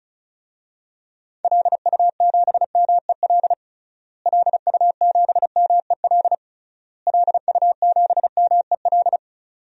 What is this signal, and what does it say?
Morse code, heard as a keyed tone, about 34 wpm, 700 Hz. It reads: LU7MEL LU7MEL LU7MEL
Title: DXCC-100 Call Signs - 6 Characters - Repeated 3x - 35wpm